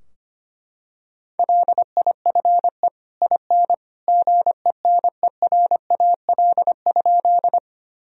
Morse code, heard as a keyed tone, about 25 wpm, 700 Hz. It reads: LIFE IN GENERAL?
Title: Sentences from Top 1000 Words - Review Part 3 - 25wpm